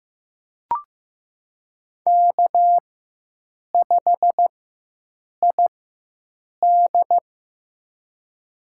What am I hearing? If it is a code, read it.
Morse code, heard as a keyed tone, about 15 wpm, spaced as for 7 wpm, 700 Hz. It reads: K5ID
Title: USA Call Signs - 4 Characters - 4x Character Spacing - 15wpm